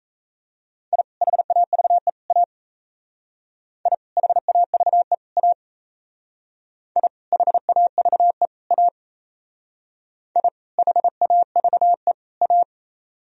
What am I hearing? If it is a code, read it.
Morse code, heard as a keyed tone, about 34 wpm, 700 Hz. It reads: I HAVE A I HAVE A I HAVE A I HAVE A
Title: Speed-Racing - Sets of 3 Words 28wpm